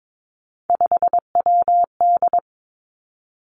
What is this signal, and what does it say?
Morse code, heard as a keyed tone, about 22 wpm, 700 Hz. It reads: 5WD
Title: Three Challenging Letter-Number Combinations 22wpm